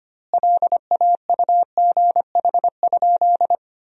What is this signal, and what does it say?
Morse code, heard as a keyed tone, about 25 wpm, 700 Hz. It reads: LAUGH?
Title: 2x Word-Spacing - Sentences from Top 500 Words - 25wpm